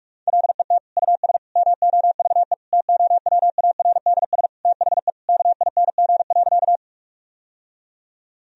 Morse code, heard as a keyed tone, about 46 wpm, 700 Hz. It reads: LET US MOVE TOWARDS THE KING.
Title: Sentences from Top 1000 Words - Review Part 2 - 45wpm